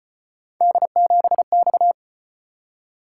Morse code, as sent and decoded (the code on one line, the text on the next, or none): -.. --... -..-
D7X